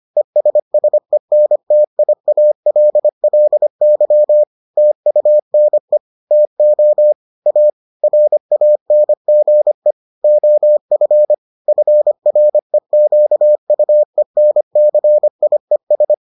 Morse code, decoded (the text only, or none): ESSENTIALLY TUNE TO A RANGE OF FREQUENCIES